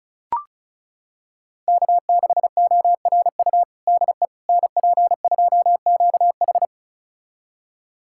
Morse code, heard as a keyed tone, about 36 wpm, 700 Hz. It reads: K6ORU DE NP2QH
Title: USA Callsign DE Callsign - 5 Characters - 35wpm